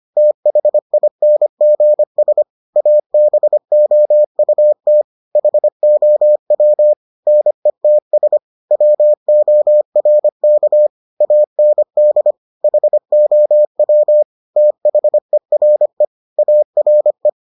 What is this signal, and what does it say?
Morse code, heard as a keyed tone, about 25 wpm, 600 Hz. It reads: THINGS ABOUT HOW NETS WORK AND HOW THERE ARE